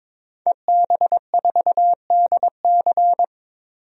Morse code, heard as a keyed tone, about 22 wpm, 700 Hz. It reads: EB4DC
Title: DXCC-100 Call Signs - 5 Characters - 22wpm